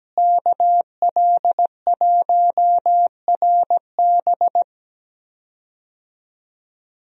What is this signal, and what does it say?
Morse code, heard as a keyed tone, about 17 wpm, 700 Hz. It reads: KL1RB
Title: USA Call Signs - 5 Characters - Repeated 3x - 17wpm